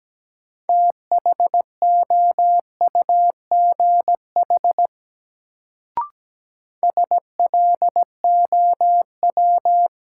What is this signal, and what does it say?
Morse code, heard as a keyed tone, about 17 wpm, 700 Hz. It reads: THOUGH SLOW
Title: Sentences from Top 400 Words - New Word Drill - 17wpm